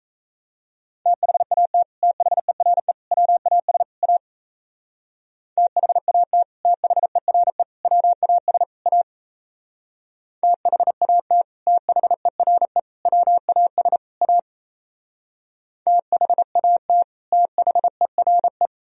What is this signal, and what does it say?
Morse code, heard as a keyed tone, about 33 wpm, 700 Hz. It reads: THAT THERE WAS A THAT THERE WAS A THAT THERE WAS A THAT THERE
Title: Speed-Racing - Sets of 4 Words 28wpm